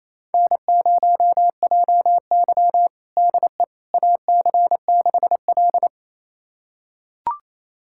Morse code, heard as a keyed tone, about 27 wpm, 700 Hz. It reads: N0JY DE AC6L
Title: USA Callsign DE Callsign - 4 Characters - 28wpm